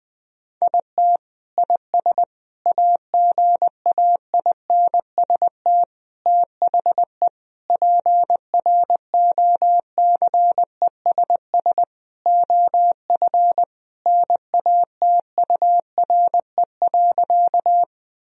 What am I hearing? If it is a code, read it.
Morse code, heard as a keyed tone, about 20 wpm, 700 Hz. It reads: IT IS AGAINST THE PROCESS OF NATURE.
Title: Sentences from Top 1000 Words - Review Part 3 - 20wpm